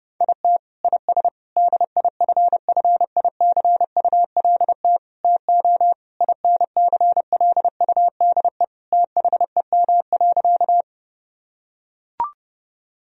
Morse code, heard as a keyed tone, about 30 wpm, 700 Hz. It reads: IT IS DIFFICULT TO INCLUDE THEM.
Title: Sentences from Top 1000 Words - Review Part 3 - 30wpm